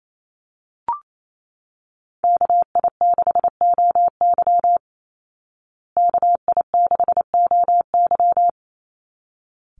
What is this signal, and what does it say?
Morse code, heard as a keyed tone, about 28 wpm, 700 Hz. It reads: KI6OY KI6OY
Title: Call Signs Repeated 3x - 28wpm